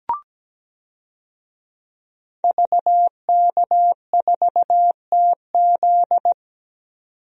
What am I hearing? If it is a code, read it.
Morse code, heard as a keyed tone, about 17 wpm, 700 Hz. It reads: VK4TZ
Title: DXCC-100 Call Signs - 5 Characters - 17wpm